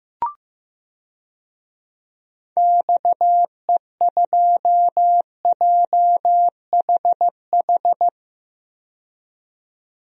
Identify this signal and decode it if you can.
Morse code, heard as a keyed tone, about 15 wpm, 700 Hz. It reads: XE2JHH
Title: DXCC-100 Call Signs - 6 Characters - 15wpm